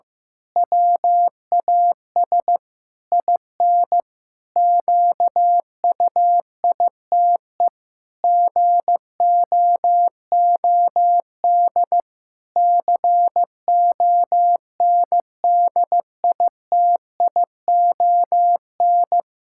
Morse code, heard as a keyed tone, about 15 wpm, 700 Hz. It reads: WAS IN QUITE GOOD CONDITION
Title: Pangram Challenge v2 - 15wpm